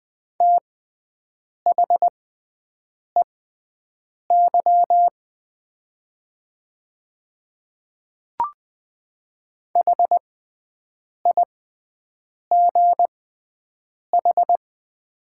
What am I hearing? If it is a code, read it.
Morse code, heard as a keyed tone, about 20 wpm, spaced as for 7 wpm, 700 Hz. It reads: THEY HIGH
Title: 6x Character Spacing - Top 200 Words - 20wpm